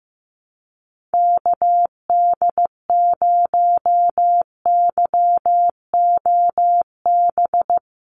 Morse code, heard as a keyed tone, about 15 wpm, 700 Hz. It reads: KD0YOB
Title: NA SOTA Activator Callsigns 15wpm